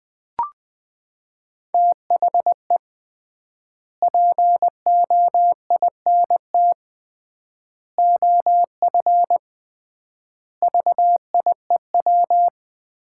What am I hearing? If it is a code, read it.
Morse code, heard as a keyed tone, about 21 wpm, 700 Hz. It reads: THE POINT OF VIEW
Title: 3x Word-Spacing - Sets of 5 Words - 20wpm